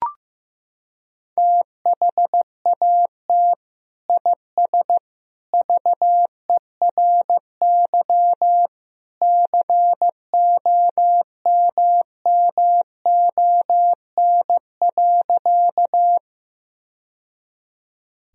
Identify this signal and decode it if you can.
Morse code, heard as a keyed tone, about 15 wpm, 700 Hz. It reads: THAT IS VERY COMMON.
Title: Sentences from Top 1000 Words - Review Part 3 - 15wpm